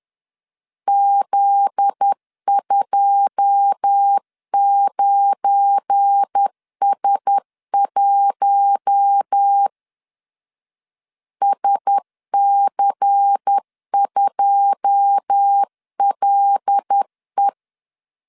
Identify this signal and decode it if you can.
Morse code, heard as a keyed tone, about 10 wpm, 800 Hz. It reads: Z29S1 SC2LE
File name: LEZIONI / 9 2 10wpm 2736